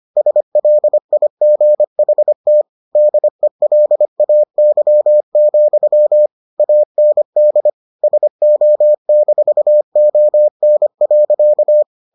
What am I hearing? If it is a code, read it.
Morse code, heard as a keyed tone, about 25 wpm, 600 Hz. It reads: SLIGHT DELAY, AND SO-ON.